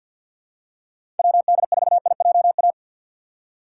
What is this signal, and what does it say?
Morse code, heard as a keyed tone, about 53 wpm, 700 Hz. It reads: WD4IJA
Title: USA Call Signs - 6 Characters - 6x Character Spacing - 50wpm